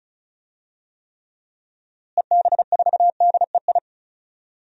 Morse code, heard as a keyed tone, about 34 wpm, 700 Hz. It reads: EB4DEI
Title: DXCC-100 Call Signs - 6 Characters - Repeated 3x - 35wpm